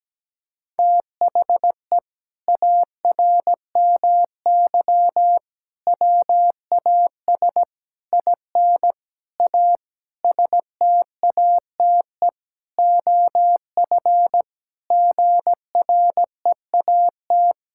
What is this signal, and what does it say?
Morse code, heard as a keyed tone, about 17 wpm, 700 Hz. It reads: THE ARMY WAS IN A STATE OF GREAT
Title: Sentences from Top 1600 Words 17wpm